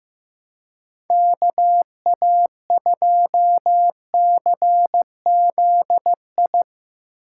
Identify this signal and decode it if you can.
Morse code, heard as a keyed tone, about 16 wpm, 700 Hz. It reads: KA2CZI
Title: Speed Racing - USA 2x3 Call Signs 15wpm